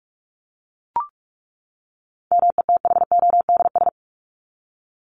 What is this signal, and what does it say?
Morse code, heard as a keyed tone, about 44 wpm, 700 Hz. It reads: METHODS
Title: ES:S Suffix - Top 100 Words with Base Words 45wpm